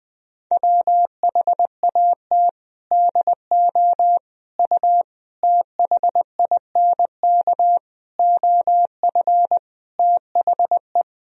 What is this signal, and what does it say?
Morse code, heard as a keyed tone, about 19 wpm, 700 Hz. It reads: WHAT DO U THINK OF THE
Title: Sentences from Top 300 Words and Abbreviations 20wpm